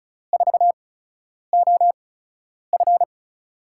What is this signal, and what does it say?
Morse code, heard as a keyed tone, about 36 wpm, spaced as for 10 wpm, 700 Hz. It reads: 4OF
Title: USA Call Signs - 8x Character Spacing - 35wpm